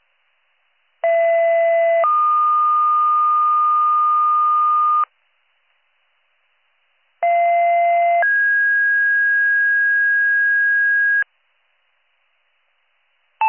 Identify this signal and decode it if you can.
Two-tone page, heard as a one-second tone, then a three-second tone, about 670 then 1220 Hz, about 690 then 1650 Hz. It is heard from 1.0 to 11.2 s.